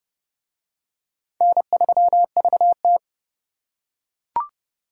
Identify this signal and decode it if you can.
Morse code, heard as a keyed tone, about 28 wpm, 700 Hz. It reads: N3VT